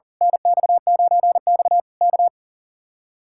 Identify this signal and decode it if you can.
Morse code, heard as a keyed tone, about 38 wpm, 700 Hz. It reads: NX9X K
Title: Speed-Racing Common QSO Phrases - 40wpm